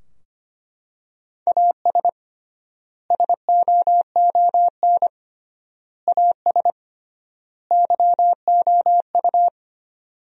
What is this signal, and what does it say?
Morse code, heard as a keyed tone, about 25 wpm, 700 Hz. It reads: AS SOON AS YOU
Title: 3x Word-Spacing - Sentences from Top 400 Words - 25wpm